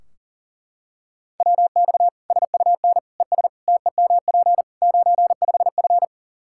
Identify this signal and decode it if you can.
Morse code, heard as a keyed tone, about 37 wpm, 700 Hz. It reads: WX SUN ES TEMP 95F
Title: Practice QSO - Exchange 2 - Info - 40wpm